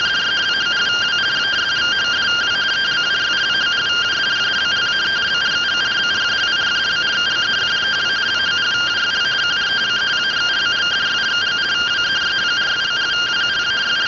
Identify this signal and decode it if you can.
Two-tone teleprinter signal, about 170 Hz shift.